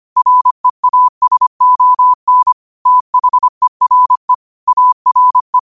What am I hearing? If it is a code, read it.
Morse code, heard as a keyed tone, about 25 wpm, 1000 Hz. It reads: REASON THERE ARE
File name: qotdfiles / 25wpm